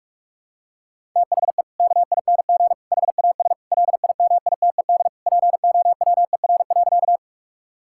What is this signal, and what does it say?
Morse code, heard as a keyed tone, about 41 wpm, 690 Hz. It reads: THE KING HAS LIMITED POWER.
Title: Sentences from Top 1000 Words - Review Part 6 - 45wpm